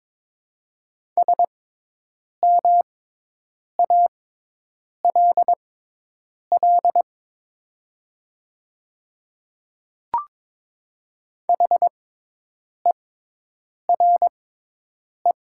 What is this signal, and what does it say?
Morse code, heard as a keyed tone, about 22 wpm, spaced as for 8 wpm, 700 Hz. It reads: SMALL HERE
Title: 6x Character Spacing - Top 200 Words - 22wpm